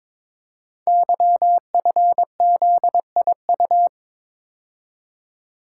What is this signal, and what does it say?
Morse code, heard as a keyed tone, about 22 wpm, 700 Hz. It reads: YFZIU
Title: Code Groups - 5 Letters - 22wpm